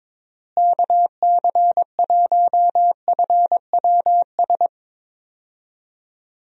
Morse code, heard as a keyed tone, about 22 wpm, 700 Hz. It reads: KC1FWS
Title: USA Call Signs - 6 Characters - Repeated 2x - 22wpm